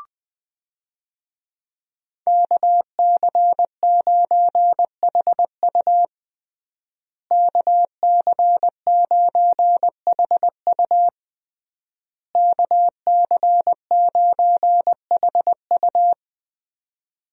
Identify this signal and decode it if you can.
Morse code, heard as a keyed tone, about 21 wpm, 700 Hz. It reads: KC9HU KC9HU KC9HU
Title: USA Call Signs - 5 Characters - Repeated 3x - 20wpm